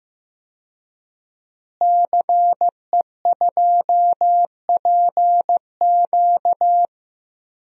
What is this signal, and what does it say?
Morse code, heard as a keyed tone, about 15 wpm, 700 Hz. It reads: CE2PQ
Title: DXCC-100 Call Signs - 5 Characters - Repeated 2x - 15wpm